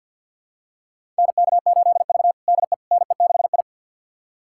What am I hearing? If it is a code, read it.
Morse code, heard as a keyed tone, about 46 wpm, 700 Hz. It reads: NK8V DE NE6I
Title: USA Callsign DE Callsign - 4 Characters - 50wpm